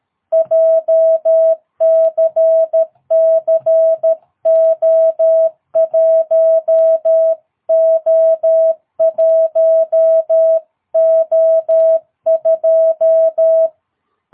Morse code, heard as a keyed tone, about 13 wpm, 660 Hz. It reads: JCC O1O1O2
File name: jh8cbh / 2022cw